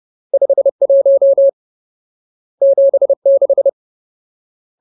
Morse code, heard as a keyed tone, about 30 wpm, 550 Hz. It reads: 51 76
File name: Ultimate Word List / 30WPM Ultimate